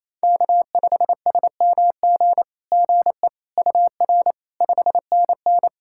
Morse code, heard as a keyed tone, about 28 wpm, 700 Hz. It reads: K5SMG GE UR 5NN
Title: SOTA QSO Practice 28wpm